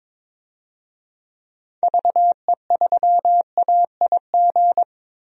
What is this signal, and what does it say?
Morse code, heard as a keyed tone, about 22 wpm, 700 Hz. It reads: VE3AIG
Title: DXCC-100 Call Signs - 6 Characters - Repeated 3x - 22wpm